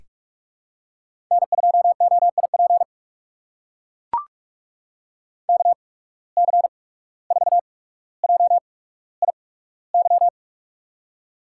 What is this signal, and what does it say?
Morse code, heard as a keyed tone, about 44 wpm, spaced as for 18 wpm, 700 Hz. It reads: N2OIP KC4JIY
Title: USA Call Signs - 8x Character Spacing - 45wpm